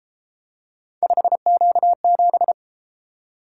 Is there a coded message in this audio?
Yes: Morse code from 1.0 to 2.5 s, heard as a keyed tone, about 30 wpm, 700 Hz. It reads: 5Q7